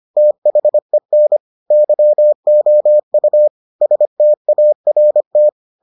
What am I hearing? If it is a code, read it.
Morse code, heard as a keyed tone, about 25 wpm, 600 Hz. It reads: THEN YOU START